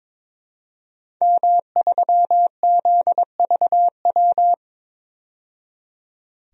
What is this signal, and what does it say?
Morse code, heard as a keyed tone, about 22 wpm, 700 Hz. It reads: M3ZVW
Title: DXCC-100 Call Signs - 5 Characters - 22wpm